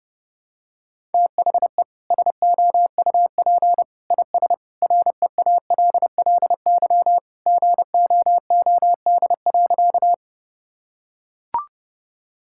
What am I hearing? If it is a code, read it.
Morse code, heard as a keyed tone, about 30 wpm, 700 Hz. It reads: THE SOUP IS REALLY GOOD.
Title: Sentences from Top 1000 Words - Review Part 6 - 30wpm